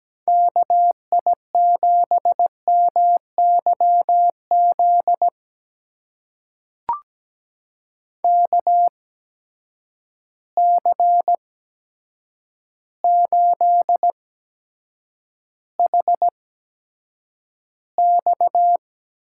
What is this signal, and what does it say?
Morse code, heard as a keyed tone, about 17 wpm, spaced as for 8 wpm, 700 Hz. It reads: KI7MYZ KC8HX
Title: USA Call Signs - 6 Characters - 8x Character Spacing - 17wpm